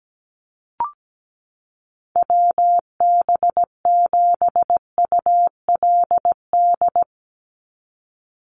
Morse code, heard as a keyed tone, about 17 wpm, 700 Hz. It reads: WB7ULD
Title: NA SOTA Chaser Callsigns 17wpm